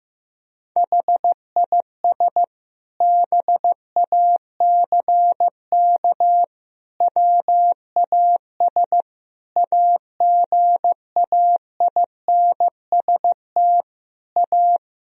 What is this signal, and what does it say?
Morse code, heard as a keyed tone, about 15 wpm, 700 Hz. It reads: HIS BACK WAS AGAINST A
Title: Sentences from Top 1000 Words - Review Part 1 - 15wpm